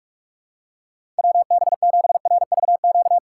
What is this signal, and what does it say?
Morse code, heard as a keyed tone, about 46 wpm, 700 Hz. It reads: WB7RVQ
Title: USA Call Signs - 4x Character Spacing - 45wpm